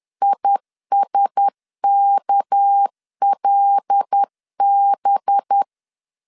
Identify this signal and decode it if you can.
Morse code, heard as a keyed tone, about 10 wpm, 800 Hz. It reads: ISKLB
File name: LEZIONI / 6 PKX0 / 6 PK 10wpm 2436